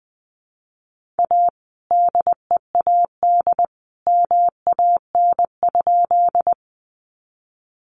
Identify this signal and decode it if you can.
Morse code, heard as a keyed tone, about 20 wpm, 700 Hz. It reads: A DEAD MAN?
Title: Sentences from Top 1500 Words 20wpm